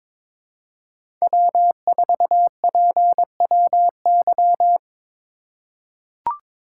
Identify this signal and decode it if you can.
Morse code, heard as a keyed tone, about 22 wpm, 700 Hz. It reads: W4PWY
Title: USA Call Signs - 5 Characters - 2x Character Spacing - 22wpm